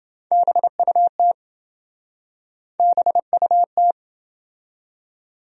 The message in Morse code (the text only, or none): BUT BUT